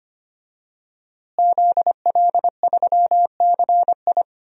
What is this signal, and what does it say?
Morse code, heard as a keyed tone, about 25 wpm, 700 Hz. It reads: ZL3CI
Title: DXCC-100 Call Signs - 5 Characters - Repeated 2x - 25wpm